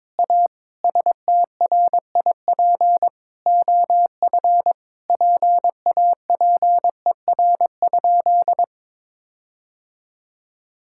Morse code, heard as a keyed tone, about 22 wpm, 700 Hz. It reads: A STRIP OF PAPER?
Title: Sentences from Top 1600 Words 22wpm